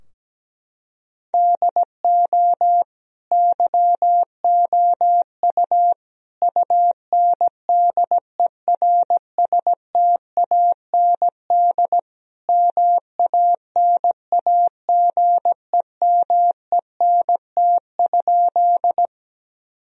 Morse code, heard as a keyed tone, about 17 wpm, 700 Hz. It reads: DO YOU UNDERSTAND MANAGEMENT?
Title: Sentences from Top 1000 Words - Review Part 4 - 17wpm